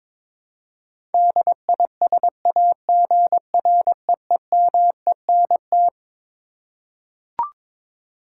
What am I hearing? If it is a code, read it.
Morse code, heard as a keyed tone, about 22 wpm, 700 Hz. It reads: DISAGREEMENT
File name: Twelve Letter Words - 22wpm